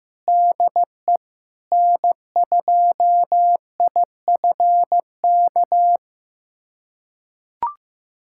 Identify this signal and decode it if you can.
Morse code, heard as a keyed tone, about 15 wpm, 700 Hz. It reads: DE N2IFK
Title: USA Callsign DE Callsign 15wpm